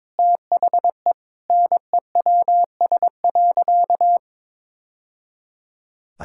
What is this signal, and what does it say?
Morse code, heard as a keyed tone, about 22 wpm, 700 Hz. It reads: THE NEWS.
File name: practice / 22wpm / Sentences with 2-Word Phrases - 22wpm